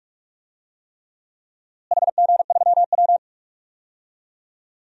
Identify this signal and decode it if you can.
Morse code, heard as a keyed tone, about 42 wpm, 700 Hz. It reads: HG3W
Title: DXCC-100 Call Signs - 4 Characters - Repeated 3x - 45wpm